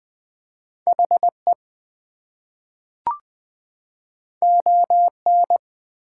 Morse code, heard as a keyed tone, about 20 wpm, 700 Hz. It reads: HE ON
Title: Two Letter Words - 20wpm